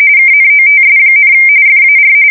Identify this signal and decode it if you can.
RTTY, about 45.45 baud. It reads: KNOTS, FAR
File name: fsk170 / usnavy4